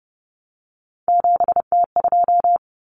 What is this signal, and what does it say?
Morse code, heard as a keyed tone, about 30 wpm, 700 Hz. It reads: 7T2
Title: Common Cut Numbers - 3-Digit 30wpm